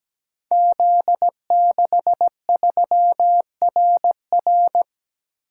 Morse code, heard as a keyed tone, about 17 wpm, 700 Hz. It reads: Z63RR
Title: Code Groups - 5 Letter:Numbers - 17wpm